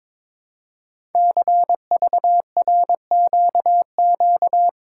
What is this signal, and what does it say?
Morse code, heard as a keyed tone, about 22 wpm, 700 Hz. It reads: CVRQQ
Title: Code Groups - 5 Letters - 22wpm